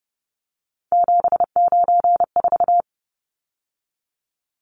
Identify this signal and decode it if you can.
Morse code, heard as a keyed tone, about 30 wpm, 700 Hz. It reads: 794
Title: Numbers - 3-Digit 30wpm